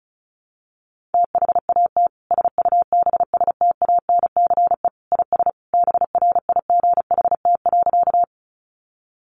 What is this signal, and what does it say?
Morse code, heard as a keyed tone, about 34 wpm, 700 Hz. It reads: THAT SUBSTANCE IS BRIGHT.
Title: Sentences from Top 1500 Words 35wpm